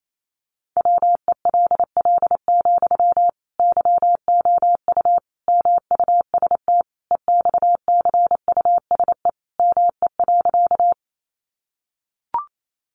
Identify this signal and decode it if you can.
Morse code, heard as a keyed tone, about 28 wpm, 700 Hz. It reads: WELL, YOU MUST EXCUSE ME.
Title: Sentences from Top 700 Words 28wpm